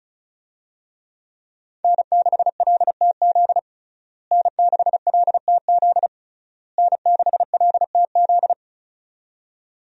Morse code, heard as a keyed tone, about 36 wpm, 700 Hz. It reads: N6LTZ N6LTZ N6LTZ
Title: USA Call Signs - 5 Characters - Repeated 3x - 35wpm